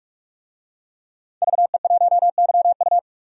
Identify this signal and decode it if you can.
Morse code, heard as a keyed tone, about 43 wpm, 700 Hz. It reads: VE1YU
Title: DXCC-100 Call Signs - 5 Characters - 45wpm